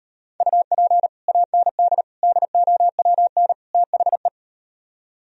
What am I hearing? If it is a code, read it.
Morse code, heard as a keyed tone, about 38 wpm, 700 Hz. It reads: UP AND DOWN THE